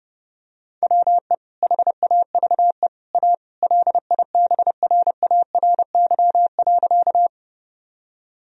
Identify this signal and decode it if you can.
Morse code, heard as a keyed tone, about 32 wpm, 700 Hz. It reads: WE HAVE A LIBRARY.
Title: Sentences from Top 1000 Words - Review Part 4 - 30wpm